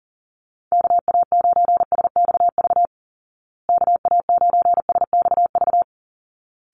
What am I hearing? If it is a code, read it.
Morse code, heard as a keyed tone, about 40 wpm, 700 Hz. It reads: KA9SXV KA9SXV
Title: Call Signs Repeated 3x - 40wpm